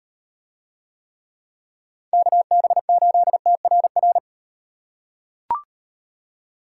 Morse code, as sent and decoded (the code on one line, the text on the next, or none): -.- -... ---.. - .-. .-.
KB8TRR